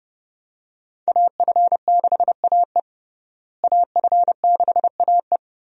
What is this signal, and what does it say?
Morse code, heard as a keyed tone, about 30 wpm, 700 Hz. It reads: AF6AE AF6AE
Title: USA Call Signs - 5 Characters - Repeated 3x - 30wpm